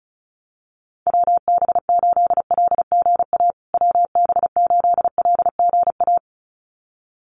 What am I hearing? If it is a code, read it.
Morse code, heard as a keyed tone, about 34 wpm, 700 Hz. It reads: WB8LGA WB8LGA
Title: USA Call Signs Repeated 2x 35wpm